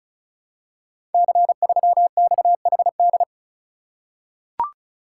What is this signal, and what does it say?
Morse code, heard as a keyed tone, about 36 wpm, 690 Hz. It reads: C3XHD
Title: Code Groups - 5 Letter:Numbers - 35wpm